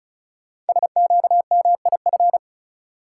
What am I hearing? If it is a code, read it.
Morse code, heard as a keyed tone, about 36 wpm, 700 Hz. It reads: SQMIF
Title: Code Groups - 5 Letters - 35wpm